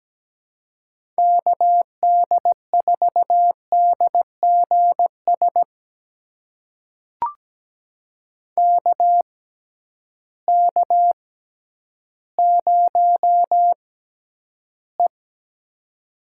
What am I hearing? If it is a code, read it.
Morse code, heard as a keyed tone, about 17 wpm, spaced as for 10 wpm, 700 Hz. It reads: KD4DGS KK0E
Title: USA Call Signs - 6x Character Spacing - 17wpm